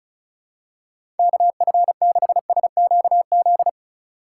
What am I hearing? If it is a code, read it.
Morse code, heard as a keyed tone, about 36 wpm, 700 Hz. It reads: KF6SQZ
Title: USA Call Signs - 6 Characters - 3x Character Spacing - 35wpm